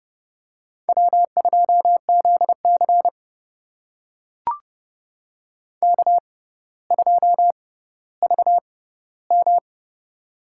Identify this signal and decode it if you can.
Morse code, heard as a keyed tone, about 32 wpm, spaced as for 16 wpm, 700 Hz. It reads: W2ZC K2VM
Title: USA Call Signs - 4 Characters - 6x Character Spacing - 30wpm